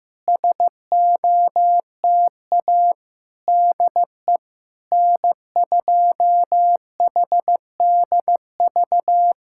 Morse code, heard as a keyed tone, about 15 wpm, 700 Hz. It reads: SOTA DE N2HDV
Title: SOTA QSO Practice 15wpm